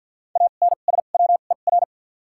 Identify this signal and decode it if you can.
Morse code, heard as a keyed tone, about 53 wpm, spaced as for 37 wpm, 700 Hz. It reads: ANSWER
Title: 2x Character Spacing - Top 300 Words - 50wpm